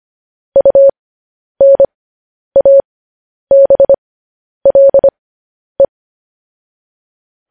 Morse code, heard as a keyed tone, about 25 wpm, spaced as for 10 wpm, 550 Hz. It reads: UNABLE
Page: news headlines in morse code at 10 wpm